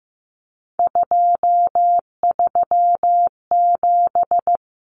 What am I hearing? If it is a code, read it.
Morse code, heard as a keyed tone, about 15 wpm, 700 Hz. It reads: 237
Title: Numbers - 3-Digit 15wpm